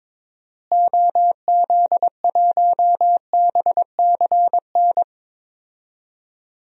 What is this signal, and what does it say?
Morse code, heard as a keyed tone, about 22 wpm, 700 Hz. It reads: OZ1BCN
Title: DXCC-100 Call Signs - 6 Characters - Repeated 3x - 22wpm